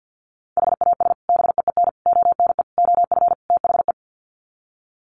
Morse code, heard as a keyed tone, about 50 wpm, 700 Hz. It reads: HAS BEEN ONE OF THE
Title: Sets of 5 Words Encore 50wpm